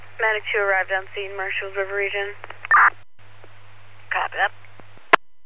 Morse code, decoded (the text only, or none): none